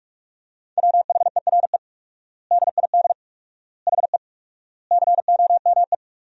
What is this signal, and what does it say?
Morse code, heard as a keyed tone, about 46 wpm, 700 Hz. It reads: WHERE DID HE COME
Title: 4x Word-Spacing - Sentences from Top 200 Words - 45wpm